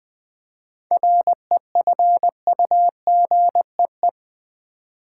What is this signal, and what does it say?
Morse code, heard as a keyed tone, about 20 wpm, 700 Hz. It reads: REFUGEE
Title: 7 Letter Words - First 3 Letters Given - 20wpm